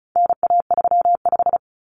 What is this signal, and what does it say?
Morse code, heard as a keyed tone, about 34 wpm, 700 Hz. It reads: NA35
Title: Common Cut Numbers - 4-Digit 35wpm